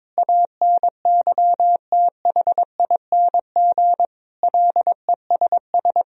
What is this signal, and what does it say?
Morse code, heard as a keyed tone, about 22 wpm, 700 Hz. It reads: ANYTHING LESS
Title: Sentences from Top 1000 Words - Review Part 2 - 22wpm